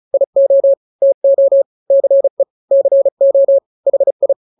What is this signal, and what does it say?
Morse code, heard as a keyed tone, about 34 wpm, 550 Hz. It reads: IO TO CE CO HI
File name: n-grams / 2-grams 35 30 WPM